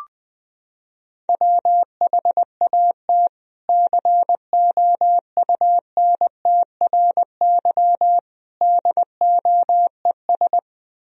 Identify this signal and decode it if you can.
Morse code, heard as a keyed tone, about 20 wpm, 700 Hz. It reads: WHAT COUNTRY DOES
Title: Sentences from Top 300 Words and Abbreviations 20wpm